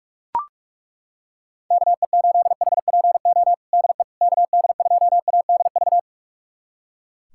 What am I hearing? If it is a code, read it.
Morse code, heard as a keyed tone, about 41 wpm, 700 Hz. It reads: KE8HPO DE KD2ADV
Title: USA Callsign DE Callsign - 6 Characters - 45wpm